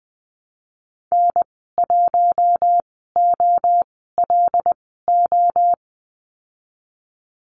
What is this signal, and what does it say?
Morse code, heard as a keyed tone, about 20 wpm, spaced as for 15 wpm, 700 Hz. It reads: N1OLO
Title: 186 Lesson 30g - Letter b - Call Signs 20wpm